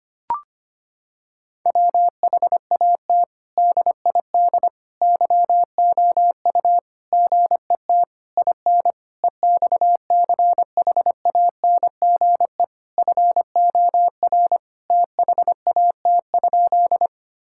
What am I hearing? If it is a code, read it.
Morse code, heard as a keyed tone, about 25 wpm, 700 Hz. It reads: WHAT DID YOU GET IN EXCHANGE FOR THAT?
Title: Sentences from Top 1000 Words - Review Part 6 - 25wpm